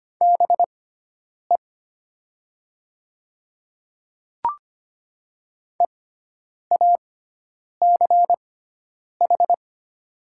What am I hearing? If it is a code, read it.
Morse code, heard as a keyed tone, about 25 wpm, spaced as for 9 wpm, 700 Hz. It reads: BE EACH
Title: 6x Character Spacing - Top 100 Words - 25wpm